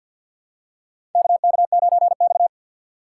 Morse code, heard as a keyed tone, about 50 wpm, 700 Hz. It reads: KK9X
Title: USA Call Signs - 2x Character Spacing - 50wpm